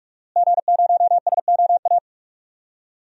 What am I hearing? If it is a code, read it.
Morse code, heard as a keyed tone, about 42 wpm, 700 Hz. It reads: G0SOA
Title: DXCC-100 Call Signs - 5 Characters - Repeated 3x - 45wpm